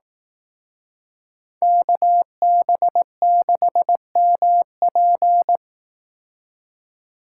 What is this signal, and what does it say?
Morse code, heard as a keyed tone, about 18 wpm, 700 Hz. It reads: KB6MP
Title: Speed Racing - USA 2x2 Call Signs 15wpm